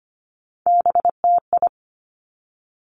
Morse code, heard as a keyed tone, about 25 wpm, 700 Hz. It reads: BTI
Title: Three Challenging Letter-Number Combinations 25wpm